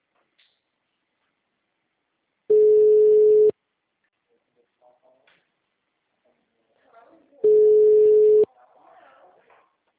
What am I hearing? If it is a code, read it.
Ringback tone (Europe, 425 Hz), with speech.
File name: out- -2005- - - .2258